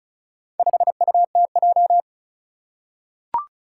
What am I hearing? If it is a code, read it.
Morse code, heard as a keyed tone, about 36 wpm, 700 Hz. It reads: 5UTJ
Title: Four Challenging Letter-Number Combinations 35wpm